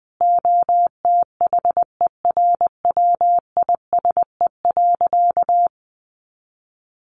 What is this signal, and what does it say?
Morse code, heard as a keyed tone, about 20 wpm, 700 Hz. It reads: OTHERWISE.
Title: Sentences from Top 1200 Words 20wpm